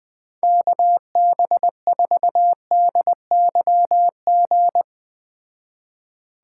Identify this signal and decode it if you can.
Morse code, heard as a keyed tone, about 21 wpm, 700 Hz. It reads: KB4DYG